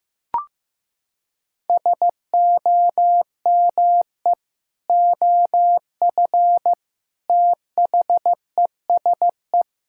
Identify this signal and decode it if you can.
Morse code, heard as a keyed tone, about 15 wpm, 700 Hz. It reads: SOME OF THESE
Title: Sentences from Top 1000 Words - Review Part 6 - 15wpm